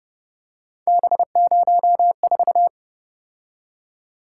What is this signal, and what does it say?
Morse code, heard as a keyed tone, about 29 wpm, 700 Hz. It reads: B04
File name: Speed-Racing - Three Letter-Number Combinations - 25wpm